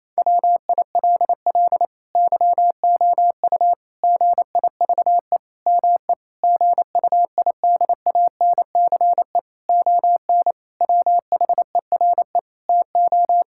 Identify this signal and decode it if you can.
Morse code, heard as a keyed tone, about 28 wpm, 700 Hz. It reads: WILL YOU GIVE ME GUIDANCE ON WHERE TO
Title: Sentences from Top 1600 Words 28wpm